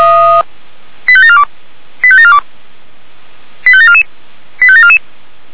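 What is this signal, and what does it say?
Five-tone selective call (ZVEI), heard as quick runs of five tones, tones of about 70 ms. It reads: 85742 85742 85744 85744
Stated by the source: ZVEI five-tone selective call received on SDR.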